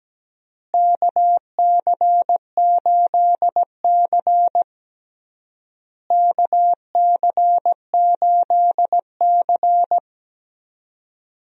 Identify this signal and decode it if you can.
Morse code, heard as a keyed tone, about 17 wpm, 700 Hz. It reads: KC8C KC8C
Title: USA Call Signs - 4 Characters - Repeated 3x - 17wpm